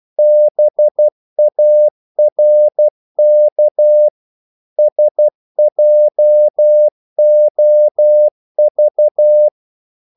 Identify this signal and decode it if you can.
Morse code, heard as a keyed tone, about 12 wpm, 600 Hz. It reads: BARK SJOV
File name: Korte ord DK 12wpm